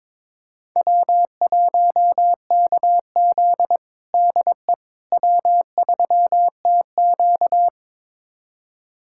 Morse code, heard as a keyed tone, about 21 wpm, 700 Hz. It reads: W1KZ DE W3TQ
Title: USA Callsign DE Callsign - 4 Characters - 22wpm